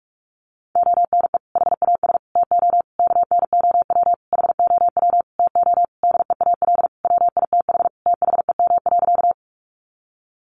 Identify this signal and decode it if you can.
Morse code, heard as a keyed tone, about 44 wpm, 700 Hz. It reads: ONE HAS TO KNOW HOW TO DEAL WITH THEM.
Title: Sentences from Top 800 Words 45wpm